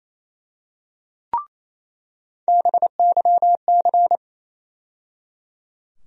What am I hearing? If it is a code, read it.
Morse code, heard as a keyed tone, about 28 wpm, 700 Hz. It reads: BYC
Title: USA Call Sign Postfixes - 3 Letters 28wpm